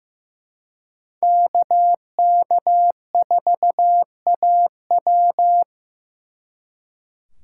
Morse code, heard as a keyed tone, about 18 wpm, 700 Hz. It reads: KK4AW
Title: Speed Racing - USA Call Signs 15wpm